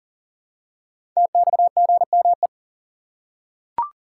Morse code, heard as a keyed tone, about 40 wpm, 700 Hz. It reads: TXGME
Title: Code Groups - 5 Letter:Numbers - 40wpm